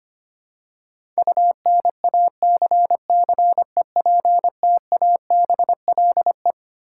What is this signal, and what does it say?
Morse code, heard as a keyed tone, about 25 wpm, 700 Hz. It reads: UNACCEPTABLE
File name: Twelve Letter Words - 25wpm